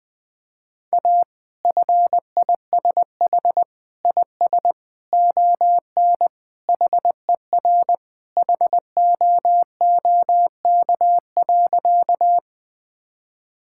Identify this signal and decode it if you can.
Morse code, heard as a keyed tone, about 20 wpm, 700 Hz. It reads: A FISH IS ON HER HOOK.
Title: Sentences from Top 1600 Words 20wpm